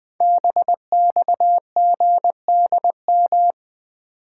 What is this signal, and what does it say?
Morse code, heard as a keyed tone, about 21 wpm, 700 Hz. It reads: BXGDM
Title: Code Groups - 5 Letters - 20wpm